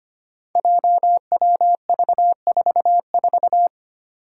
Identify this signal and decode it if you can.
Morse code, heard as a keyed tone, about 25 wpm, 700 Hz. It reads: JWV44
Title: Five Challenging Letter-Number Combinations 25wpm